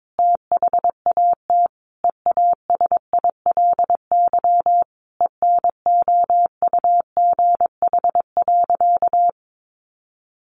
Morse code, heard as a keyed tone, about 22 wpm, 700 Hz. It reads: THAT EASILY ENOUGH.
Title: Sentences from Top 900 Words 22wpm